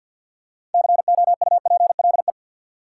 Morse code, heard as a keyed tone, about 53 wpm, 700 Hz. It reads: COUPLE
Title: Six Letter Words - 50wpm